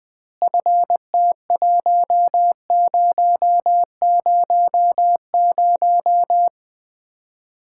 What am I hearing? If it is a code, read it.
Morse code, heard as a keyed tone, about 20 wpm, 700 Hz. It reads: FT1000
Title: Common QSO Phrases - 2x Word-Spacing - 20wpm